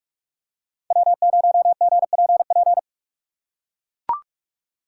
Morse code, heard as a keyed tone, about 44 wpm, 700 Hz. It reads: W0GPP
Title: USA Call Signs - 5 Characters - 6x Character Spacing - 45wpm